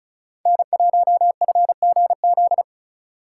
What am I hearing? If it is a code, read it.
Morse code, heard as a keyed tone, about 45 wpm, 700 Hz. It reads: N1FGZ